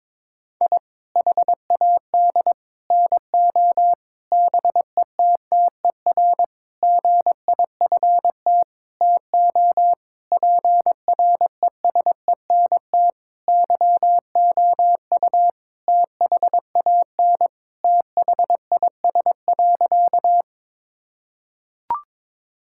Morse code, heard as a keyed tone, about 22 wpm, 700 Hz. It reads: I HAD NO BETTER GIFT TO PRESENT YOU THAN THIS.
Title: Sentences from Top 1000 Words - Review Part 6 - 22wpm